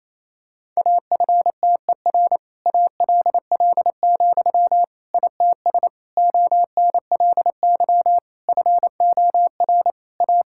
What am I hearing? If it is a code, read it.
Morse code, heard as a keyed tone, about 27 wpm, 700 Hz. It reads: AFTER ALL, ITS ONLY FOR A
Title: Sentences from Top 1000 Words - Review Part 5 - 28wpm